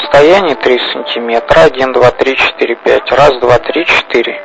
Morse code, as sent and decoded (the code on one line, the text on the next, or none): none